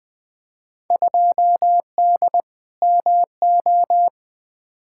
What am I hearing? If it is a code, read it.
Morse code, heard as a keyed tone, about 21 wpm, 700 Hz. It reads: 2D MO
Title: Field Day Exchanges - 20wpm